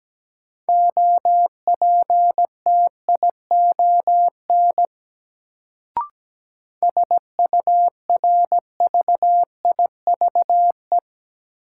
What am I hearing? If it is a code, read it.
Morse code, heard as a keyed tone, about 17 wpm, 700 Hz. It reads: OPTION SURVIVE
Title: Sentences from Top 1300 Words - New Word Drill - 17wpm